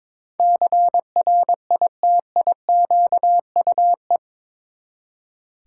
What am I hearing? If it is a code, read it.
Morse code, heard as a keyed tone, about 22 wpm, 700 Hz. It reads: CRITIQUE
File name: practice / 22wpm / Eight Letter Words - 22wpm